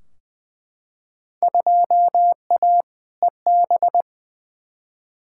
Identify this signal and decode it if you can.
Morse code, heard as a keyed tone, about 20 wpm, 700 Hz. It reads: 2A EB
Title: Field Day Exchanges - 20wpm